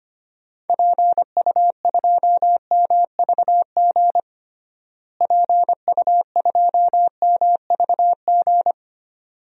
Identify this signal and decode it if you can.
Morse code, heard as a keyed tone, about 25 wpm, 700 Hz. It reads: PU2MVG PU2MVG
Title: DXCC-100 Call Signs - 6 Characters - Repeated 2x - 25wpm